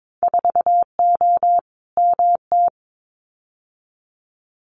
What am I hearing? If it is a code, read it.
Morse code, heard as a keyed tone, about 22 wpm, 700 Hz. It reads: 4O MT
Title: Winter Field Day Exchanges Encore 22wpm